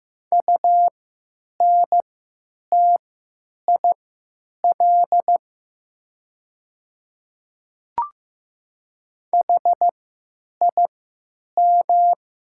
Morse code, heard as a keyed tone, about 15 wpm, spaced as for 9 wpm, 700 Hz. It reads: UNTIL HIM
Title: 3x Character Spacing - Top 300 Words - 15wpm